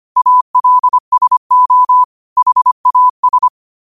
Morse code, heard as a keyed tone, about 25 wpm, 1000 Hz. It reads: ALSO HAS